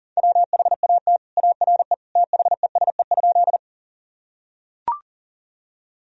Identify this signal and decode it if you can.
Morse code, heard as a keyed tone, about 37 wpm, 700 Hz. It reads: WHAT ARE THESE?
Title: Sentences from Top 1000 Words - Review Part 1 - 40wpm